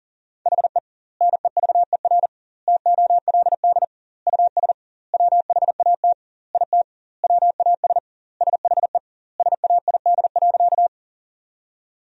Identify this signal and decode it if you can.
Morse code, heard as a keyed tone, about 40 wpm, 700 Hz. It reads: HE NEVER TOLD US WHAT IT WAS SHE SAID.
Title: 2x Word-Spacing - Sentences from Top 400 Words - 40wpm